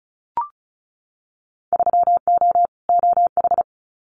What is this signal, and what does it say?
Morse code, heard as a keyed tone, about 36 wpm, 700 Hz. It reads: 3O OH
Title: Winter Field Day Exchanges 35wpm